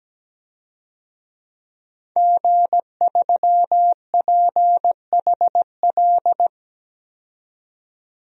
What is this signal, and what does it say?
Morse code, heard as a keyed tone, about 17 wpm, 700 Hz. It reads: G3PHL
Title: DXCC-100 Call Signs - 5 Characters - 17wpm